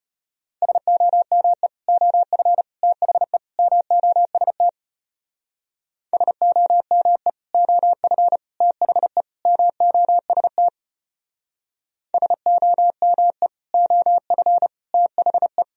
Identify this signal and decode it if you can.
Morse code, heard as a keyed tone, about 31 wpm, 700 Hz. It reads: SOME OF THE MOST SOME OF THE MOST SOME OF THE
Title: Speed-Racing - Sets of 4 Words 25wpm